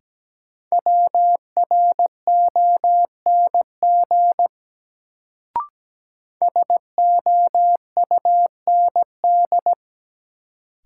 Morse code, heard as a keyed tone, about 17 wpm, 700 Hz. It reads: WRONG SOUND
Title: Sentences from Top 900 Words - New Word Drill - 17wpm